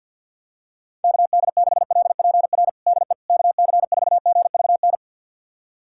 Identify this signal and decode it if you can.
Morse code, heard as a keyed tone, about 47 wpm, 700 Hz. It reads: KD6LPR DE KC4GVN
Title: USA Callsign DE Callsign - 6 Characters - 50wpm